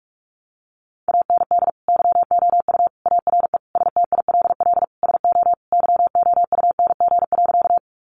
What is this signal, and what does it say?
Morse code, heard as a keyed tone, about 44 wpm, 700 Hz. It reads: AND YOU ARE STILL SO YOUNG.
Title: Sentences from Top 400 Words 45wpm